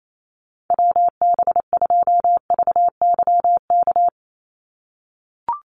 Morse code, heard as a keyed tone, about 28 wpm, 700 Hz. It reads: WB2VYK
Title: Call Signs - 28wpm